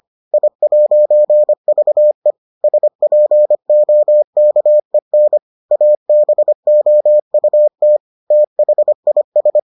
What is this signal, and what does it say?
Morse code, heard as a keyed tone, about 25 wpm, 600 Hz. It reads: I'VE SPOKEN ABOUT THIS